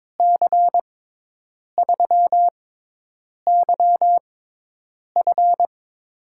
Morse code, heard as a keyed tone, about 22 wpm, spaced as for 8 wpm, 700 Hz. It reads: C3YF
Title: USA Call Signs - 5 Characters - 6x Character Spacing - 22wpm